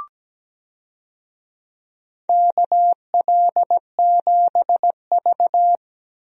Morse code, heard as a keyed tone, about 17 wpm, 700 Hz. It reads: KL7V
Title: DXCC-100 Call Signs - 4 Characters - Repeated 2x - 17wpm